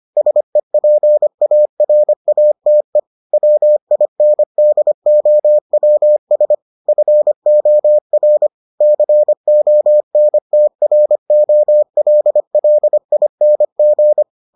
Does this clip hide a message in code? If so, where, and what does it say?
Yes: Morse code from 0.2 to 14.2 s, heard as a keyed tone, about 25 wpm, 600 Hz. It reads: SEPARATE WINDOWS FOR CONTROLLING